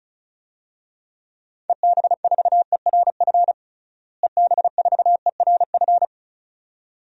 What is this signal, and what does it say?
Morse code, heard as a keyed tone, about 36 wpm, 700 Hz. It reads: EB4ERF EB4ERF
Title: DXCC-100 Call Signs - 6 Characters - Repeated 2x - 35wpm